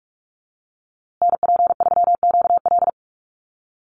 Morse code, heard as a keyed tone, about 44 wpm, 700 Hz. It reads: NP3QL
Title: USA Call Signs Repeated 3x 45wpm